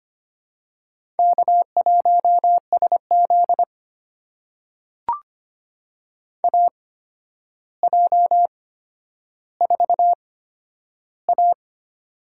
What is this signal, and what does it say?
Morse code, heard as a keyed tone, about 25 wpm, spaced as for 11 wpm, 700 Hz. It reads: K1SZ AJ4A
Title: USA Call Signs - 4 Characters - 8x Character Spacing - 25wpm